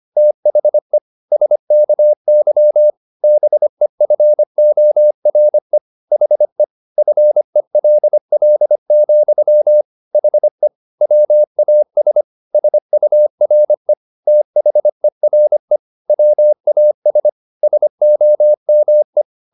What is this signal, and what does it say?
Morse code, heard as a keyed tone, about 25 wpm, 600 Hz. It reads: THE SKY BEFORE HE FELL, HE WAS SURE THERE WAS SOME